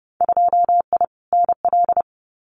Morse code, heard as a keyed tone, about 30 wpm, 700 Hz. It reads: 2I NL
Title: Winter Field Day Exchanges Encore 30wpm